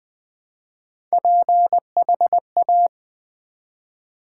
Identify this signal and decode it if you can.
Morse code, heard as a keyed tone, about 20 wpm, 700 Hz. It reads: PHA
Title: Build A Word - 1 Letter At A Time - 5-Letter Words - 20wpm